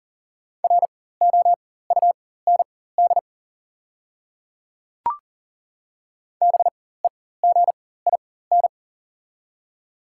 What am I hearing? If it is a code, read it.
Morse code, heard as a keyed tone, about 40 wpm, spaced as for 19 wpm, 690 Hz. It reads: ROUND BEGIN
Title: 4x Character Spacing - Top 500 Words - 40wpm